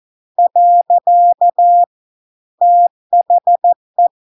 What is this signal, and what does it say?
Morse code, heard as a keyed tone, about 14 wpm, 700 Hz. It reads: . THE